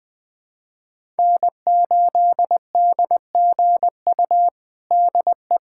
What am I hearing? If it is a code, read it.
Morse code, heard as a keyed tone, about 20 wpm, 700 Hz. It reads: N8DGU DE
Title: USA Callsign DE Callsign - 5 Characters - 20wpm